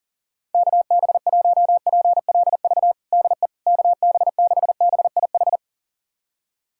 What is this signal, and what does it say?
Morse code, heard as a keyed tone, about 40 wpm, 700 Hz. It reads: KB1PLV DE KB6BIH
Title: USA Callsign DE Callsign - 6 Characters - 40wpm